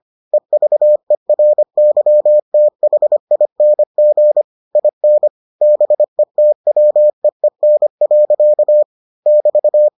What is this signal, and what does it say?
Morse code, heard as a keyed tone, about 25 wpm, 600 Hz. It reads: EVERYTHING IN BETWEEN. <BT>